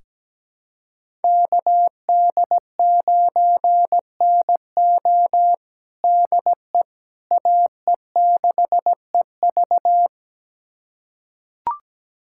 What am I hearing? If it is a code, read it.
Morse code, heard as a keyed tone, about 17 wpm, 700 Hz. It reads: KD9NO DE AE6EV
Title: USA Callsign DE Callsign - 5 Characters - 17wpm